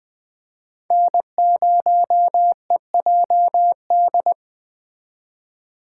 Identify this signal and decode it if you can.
Morse code, heard as a keyed tone, about 20 wpm, 700 Hz. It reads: N0EJD